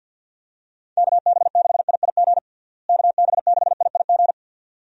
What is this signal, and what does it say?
Morse code, heard as a keyed tone, about 49 wpm, 700 Hz. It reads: KB6IIG KB6IIG
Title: USA Call Signs - 6 Characters - Repeated 2x - 50wpm